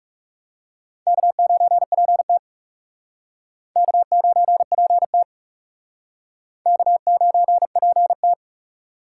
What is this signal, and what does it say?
Morse code, heard as a keyed tone, about 38 wpm, 700 Hz. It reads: K9PT K9PT K9PT